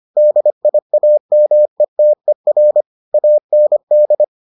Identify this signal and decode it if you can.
Morse code, heard as a keyed tone, about 25 wpm, 600 Hz. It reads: DIAMETER AND